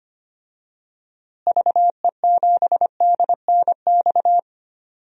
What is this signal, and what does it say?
Morse code, heard as a keyed tone, about 25 wpm, 700 Hz. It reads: VE7DNX
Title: DXCC-100 Call Signs - 6 Characters - Repeated 2x - 25wpm